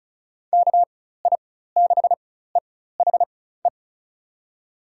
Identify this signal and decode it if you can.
Morse code, heard as a keyed tone, about 36 wpm, spaced as for 16 wpm, 700 Hz. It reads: KI6EHE
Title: USA Call Signs - 6 Characters - 4x Character Spacing - 35wpm